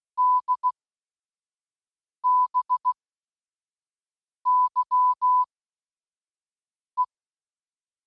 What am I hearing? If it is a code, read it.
Morse code, heard as a keyed tone, about 16 wpm, 1000 Hz. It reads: D B Y E